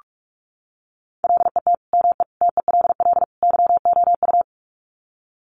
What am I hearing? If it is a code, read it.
Morse code, heard as a keyed tone, about 44 wpm, 700 Hz. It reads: LET ME TELL YOU
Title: Sets of 4 Words 45wpm